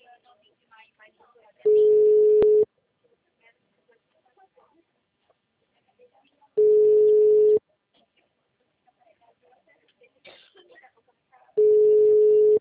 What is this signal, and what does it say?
Ringback tone (Europe, 425 Hz), with speech.